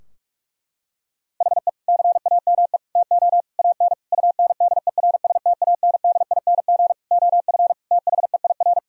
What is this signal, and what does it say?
Morse code, heard as a keyed tone, about 46 wpm, 700 Hz. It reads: HE CAME TO AN UNDERSTANDING OF THEIR
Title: Sentences from Top 1000 Words - Review Part 3 - 45wpm